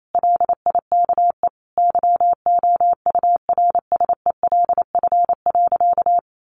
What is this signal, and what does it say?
Morse code, heard as a keyed tone, about 28 wpm, 700 Hz. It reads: LIKE YOURSELF.
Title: Sentences from Top 600 Words 28wpm